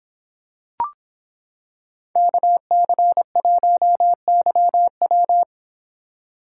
Morse code, heard as a keyed tone, about 26 wpm, 700 Hz. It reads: KC1YW